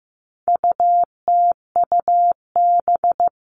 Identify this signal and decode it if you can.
Morse code, heard as a keyed tone, about 15 wpm, 700 Hz. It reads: UTUB
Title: All Cut Numbers - 4-Digit 15wpm